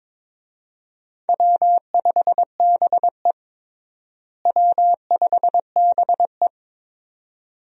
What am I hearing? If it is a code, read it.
Morse code, heard as a keyed tone, about 22 wpm, 700 Hz. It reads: W5BE W5BE
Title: USA Call Signs - 4 Characters - Repeated 2x - 22wpm